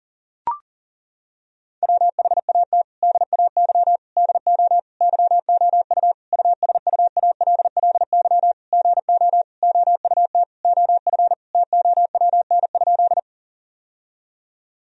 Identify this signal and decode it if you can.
Morse code, heard as a keyed tone, about 40 wpm, 700 Hz. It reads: WHAT DAY DO YOU USUALLY GO OUT OF TOWN?
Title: Sentences from Top 1000 Words - Review Part 3 - 40wpm